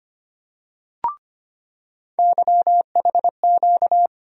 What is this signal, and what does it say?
Morse code, heard as a keyed tone, about 25 wpm, 700 Hz. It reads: YHQ
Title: USA Call Sign Postfixes - 3 Letters 25wpm